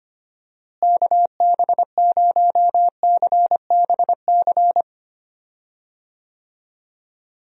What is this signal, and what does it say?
Morse code, heard as a keyed tone, about 25 wpm, 700 Hz. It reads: KB0CBC
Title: USA Call Signs - 6 Characters - Repeated 2x - 25wpm